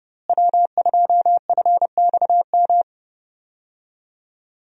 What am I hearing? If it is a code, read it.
Morse code, heard as a keyed tone, about 30 wpm, 700 Hz. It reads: W2FXM
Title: USA Call Signs - 5 Characters - Repeated 2x - 30wpm